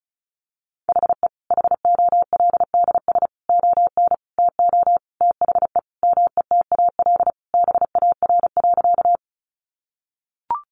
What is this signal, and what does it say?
Morse code, heard as a keyed tone, about 34 wpm, 700 Hz. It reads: HE HOLDS ON TO THE METAL BAR.
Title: Additional Easy Phrases 35wpm